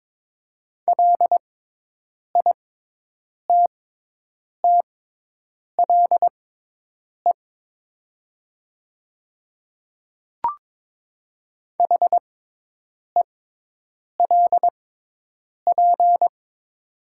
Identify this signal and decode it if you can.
Morse code, heard as a keyed tone, about 22 wpm, spaced as for 8 wpm, 700 Hz. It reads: LITTLE HELP
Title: 6x Character Spacing - Top 500 Words - 22wpm